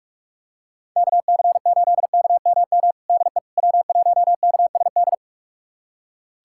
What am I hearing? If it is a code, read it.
Morse code, heard as a keyed tone, about 42 wpm, 700 Hz. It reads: KC8KMM DE W1KSD
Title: USA Callsign DE Callsign 45wpm